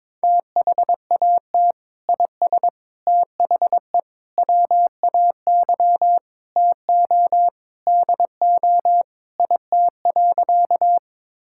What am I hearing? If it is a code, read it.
Morse code, heard as a keyed tone, about 22 wpm, 700 Hz. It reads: THAT IS THE WAY TO DO IT.
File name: practice / 22wpm / Sentences with 3-Word Phrases - 22wpm